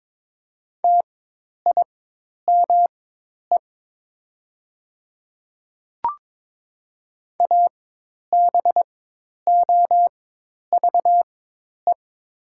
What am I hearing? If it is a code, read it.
Morse code, heard as a keyed tone, about 23 wpm, spaced as for 10 wpm, 700 Hz. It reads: TIME ABOVE
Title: 4x Character Spacing - Top 500 Words - 22wpm